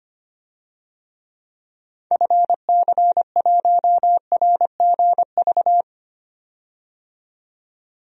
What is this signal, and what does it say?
Morse code, heard as a keyed tone, about 25 wpm, 700 Hz. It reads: FC1RGV
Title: DXCC-100 Call Signs - 6 Characters - 25wpm